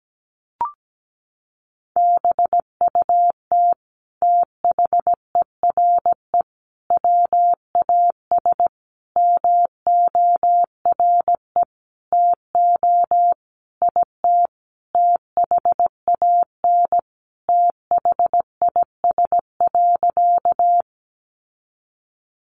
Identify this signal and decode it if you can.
Morse code, heard as a keyed tone, about 17 wpm, 700 Hz. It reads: BUT THERE WAS MORE TO IT THAN THIS.
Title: Sentences from Top 100 Words 17wpm